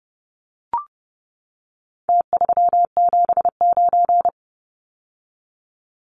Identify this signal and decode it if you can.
Morse code, heard as a keyed tone, about 30 wpm, 700 Hz. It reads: T379
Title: Numbers - Contest Style - 4-Digit 30wpm